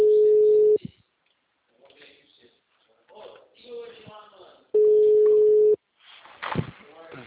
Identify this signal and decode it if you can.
Ringback tone (Europe, 425 Hz), with speech.